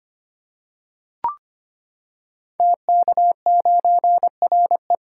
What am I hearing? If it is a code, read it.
Morse code, heard as a keyed tone, about 25 wpm, 700 Hz. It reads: TK9RE
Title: Code Groups - 5 Letter:Numbers - 25wpm